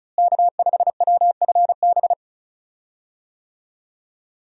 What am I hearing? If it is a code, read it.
Morse code, heard as a keyed tone, about 35 wpm, 700 Hz. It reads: K5WFB
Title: USA Call Signs - 5 Characters - Repeated 2x - 35wpm